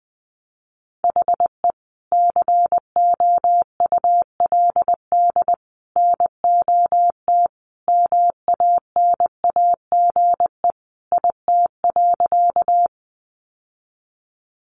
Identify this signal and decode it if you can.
Morse code, heard as a keyed tone, about 20 wpm, 700 Hz. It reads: HE COULD NOT MANAGE IT.
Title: Sentences from Top 1400 Words 20wpm